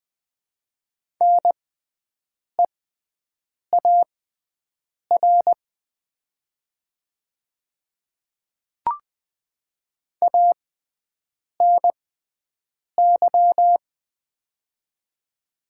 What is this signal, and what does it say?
Morse code, heard as a keyed tone, about 20 wpm, spaced as for 7 wpm, 700 Hz. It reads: NEAR ANY
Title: 6x Character Spacing - Top 300 Words - 20wpm